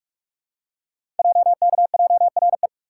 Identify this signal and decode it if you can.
Morse code, heard as a keyed tone, about 44 wpm, 700 Hz. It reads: JKJRE
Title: Code Groups - 5 Letters - 45wpm